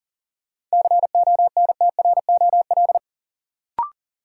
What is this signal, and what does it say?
Morse code, heard as a keyed tone, about 40 wpm, 700 Hz. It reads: CONTROL
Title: Sentences from Top 1000 Words - Review Part 2 - New Word Drill - 40wpm